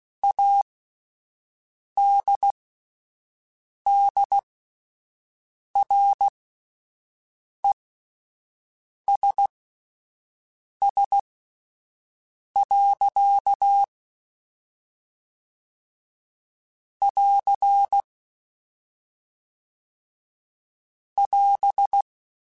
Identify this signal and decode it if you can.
Morse code, heard as a keyed tone, about 16 wpm, spaced as for 5 wpm, 800 Hz. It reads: ADDRESS. <AR> <AS>
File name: Spaced / Chuckles Spaced-mp3s / Temptation sp @10